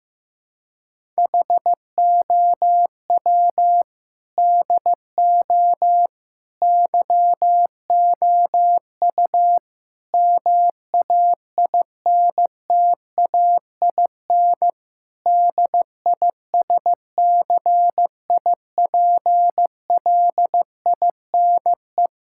Morse code, heard as a keyed tone, about 15 wpm, 700 Hz. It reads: HOW DO YOU MAINTAIN DISCIPLINE
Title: Sentences from Top 1000 Words - Review Part 6 - 15wpm